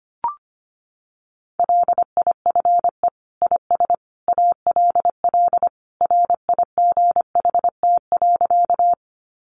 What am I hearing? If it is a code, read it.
Morse code, heard as a keyed tone, about 25 wpm, 700 Hz. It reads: LIFE IS ALL RIGHT.
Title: Sentences from Top 300 Words 25wpm